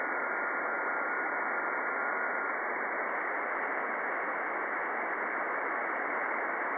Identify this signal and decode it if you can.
FT8.
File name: contests / 2019-june=vhf-k5tr- -0101 0-20